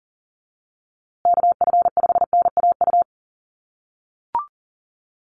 Morse code, heard as a keyed tone, about 39 wpm, 700 Hz. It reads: KF5NAU
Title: USA Call Signs Repeated 2x 40wpm